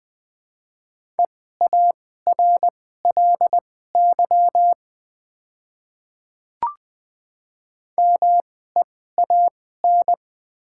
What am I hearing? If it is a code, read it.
Morse code, heard as a keyed tone, about 20 wpm, spaced as for 14 wpm, 700 Hz. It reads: EARLY MEAN
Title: 2x Character Spacing - Top 500 Words - 20wpm